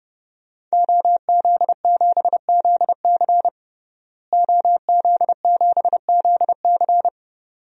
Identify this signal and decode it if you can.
Morse code, heard as a keyed tone, about 32 wpm, 700 Hz. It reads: OZ7ZC OZ7ZC
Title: DXCC-100 Call Signs - 5 Characters - Repeated 2x - 30wpm